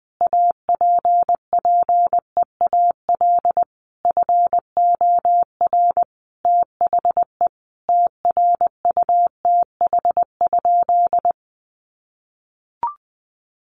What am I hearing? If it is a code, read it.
Morse code, heard as a keyed tone, about 20 wpm, 700 Hz. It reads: APPEAL FOR THE TRUTH?
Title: Sentences from Top 1300 Words 20wpm